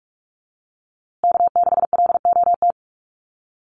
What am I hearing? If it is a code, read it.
Morse code, heard as a keyed tone, about 44 wpm, 700 Hz. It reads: K6LOT
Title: USA Call Signs Repeated 2x 45wpm